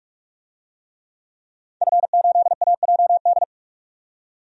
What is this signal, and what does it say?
Morse code, heard as a keyed tone, about 45 wpm, 700 Hz. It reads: F8AJD
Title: DXCC-100 Call Signs - 5 Characters - 45wpm